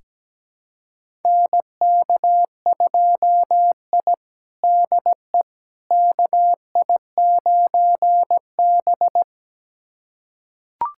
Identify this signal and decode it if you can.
Morse code, heard as a keyed tone, about 17 wpm, 700 Hz. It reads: NK2I DE KI9B
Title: USA Callsign DE Callsign - 4 Characters - 17wpm